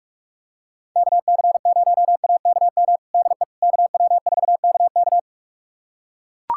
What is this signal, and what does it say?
Morse code, heard as a keyed tone, about 41 wpm, 700 Hz. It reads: KC0AKM DE KW4KK
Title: USA Callsign DE Callsign 45wpm